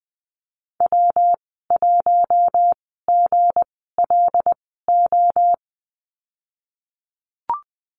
Morse code, heard as a keyed tone, about 20 wpm, spaced as for 15 wpm, 700 Hz. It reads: W1GLO
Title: 186 Lesson 30g - Letter b - Call Signs 20wpm